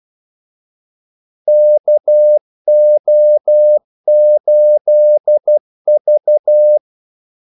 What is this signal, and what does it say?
Morse code, heard as a keyed tone, about 12 wpm, 600 Hz. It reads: KO8V
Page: 2 ekstra ordmellemrum mellem kaldesignalerne